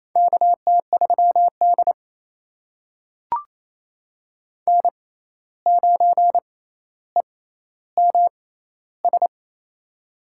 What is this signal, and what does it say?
Morse code, heard as a keyed tone, about 28 wpm, spaced as for 13 wpm, 700 Hz. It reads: KT3D N9EMS
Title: USA Call Signs - 6x Character Spacing - 28wpm